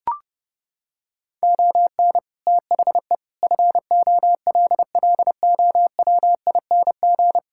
Morse code, heard as a keyed tone, about 30 wpm, 700 Hz. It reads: ON THE FOLLOWING
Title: Sentences from Top 1000 Words - Review Part 6 - 30wpm